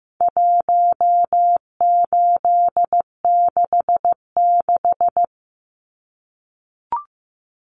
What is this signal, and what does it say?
Morse code, heard as a keyed tone, about 15 wpm, 700 Hz. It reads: 1866
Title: Numbers - 4-Digit 15wpm